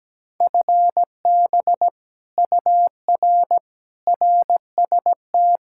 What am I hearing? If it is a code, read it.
Morse code, heard as a keyed tone, about 17 wpm, 700 Hz. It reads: FB UR RST
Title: Speed-Racing Common QSO Phrases - 17wpm